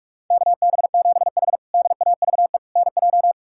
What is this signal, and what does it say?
Morse code, heard as a keyed tone, about 46 wpm, 700 Hz. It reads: KB7H DAVE NJ
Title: K1USN SST QSO Practice - 45wpm